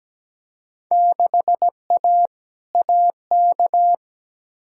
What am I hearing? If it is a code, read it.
Morse code, heard as a keyed tone, about 17 wpm, 700 Hz. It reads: 6A AK
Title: Field Day Exchanges Encore - 17wpm